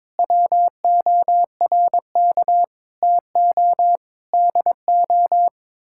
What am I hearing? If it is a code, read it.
Morse code, heard as a keyed tone, about 22 wpm, 700 Hz. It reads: WORK TO DO
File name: practice / 22wpm / Sets of 6 Words - 22wpm